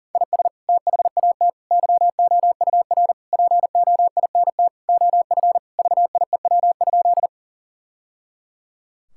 Morse code, heard as a keyed tone, about 40 wpm, 700 Hz. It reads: IS THAT YOUR POINT OF VIEW?
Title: Sentences from Top 1000 Words - Review Part 6 - 40wpm